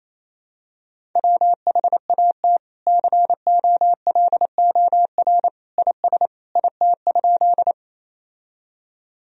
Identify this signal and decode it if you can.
Morse code, heard as a keyed tone, about 28 wpm, 700 Hz. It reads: WHAT COLOR IS IT?
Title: Sentences from Top 1000 Words - Review Part 2 - 28wpm